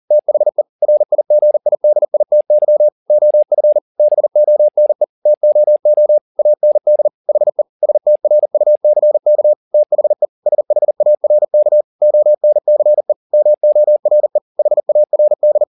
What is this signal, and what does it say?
Morse code, heard as a keyed tone, about 40 wpm, 600 Hz. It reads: THE RIGIDITY OF BONE TOO AND HE STRUCK THE SHARK ONCE MORE HARD
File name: the old man and the sea 2 40wpm